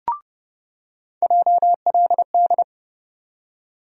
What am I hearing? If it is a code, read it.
Morse code, heard as a keyed tone, about 30 wpm, 700 Hz. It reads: JLD
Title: USA Call Sign Postfixes - 3 Letters 30wpm